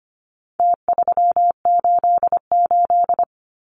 Morse code, heard as a keyed tone, about 25 wpm, 700 Hz. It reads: T388
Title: Numbers - Contest Style - 4-Digit 25wpm